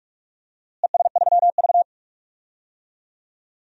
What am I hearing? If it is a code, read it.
Morse code, heard as a keyed tone, about 42 wpm, 700 Hz. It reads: ES3V
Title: DXCC-100 Call Signs - 4 Characters - Repeated 2x - 45wpm